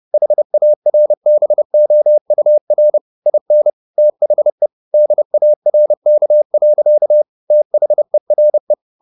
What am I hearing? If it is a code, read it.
Morse code, heard as a keyed tone, about 30 wpm, 600 Hz. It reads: HARBOUR IN THE DARK. THERE
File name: the old man and the sea / the old man and the sea 1 30wpm